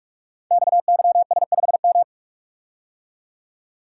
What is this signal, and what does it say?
Morse code, heard as a keyed tone, about 44 wpm, 700 Hz. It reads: XYS5M
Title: Code Groups - 5 Letter:Numbers - 45wpm